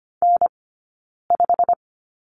Morse code, heard as a keyed tone, about 25 wpm, 700 Hz. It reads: N 5
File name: course / 25:10wpm / 092 Lesson 16d - Letter c - Two Characters 25wpm